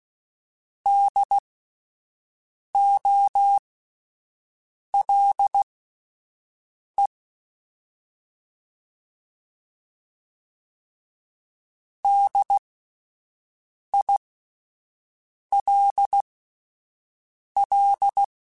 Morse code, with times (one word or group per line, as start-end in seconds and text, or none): 0.9-7.1 DOLE
12.0-18.2 DILL